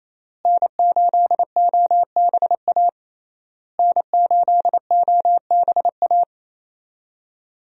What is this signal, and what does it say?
Morse code, heard as a keyed tone, about 28 wpm, 700 Hz. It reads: N8OBA N8OBA
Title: USA Call Signs - 5 Characters - Repeated 3x - 28wpm